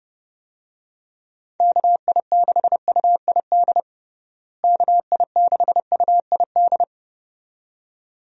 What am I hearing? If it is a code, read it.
Morse code, heard as a keyed tone, about 30 wpm, 700 Hz. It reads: KI6UID KI6UID
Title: USA Call Signs - 6 Characters - Repeated 2x - 30wpm